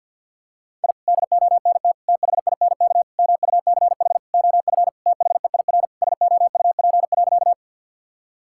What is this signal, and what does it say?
Morse code, heard as a keyed tone, about 50 wpm, 700 Hz. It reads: I DONT THINK MUCH OF THEIR SOUP.
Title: Sentences from Top 1000 Words - Review Part 6 - 50wpm